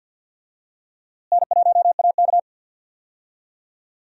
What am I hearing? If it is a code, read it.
Morse code, heard as a keyed tone, about 51 wpm, 700 Hz. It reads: N1AK
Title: USA Call Signs - 4 Characters - Repeated 3x - 50wpm